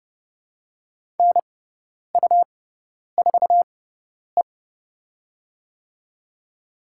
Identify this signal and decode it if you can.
Morse code, heard as a keyed tone, about 30 wpm, spaced as for 10 wpm, 700 Hz. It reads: NU4E
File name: Lesson 18g - Letter m - Call Signs 30wpm